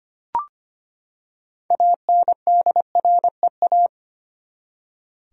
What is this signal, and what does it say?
Morse code, heard as a keyed tone, about 25 wpm, 700 Hz. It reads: ANDREA
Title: USA Common Names - 25wpm